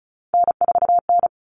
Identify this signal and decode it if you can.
Morse code, heard as a keyed tone, about 34 wpm, 700 Hz. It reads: N4N
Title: Common Cut Numbers - 3-Digit 35wpm